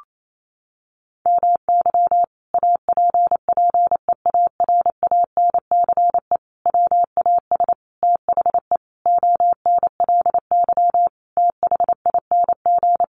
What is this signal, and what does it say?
Morse code, heard as a keyed tone, about 28 wpm, 700 Hz. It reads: MY APPEARANCE WAS THE ONLY THING
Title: Sentences from Top 1100 Words 28wpm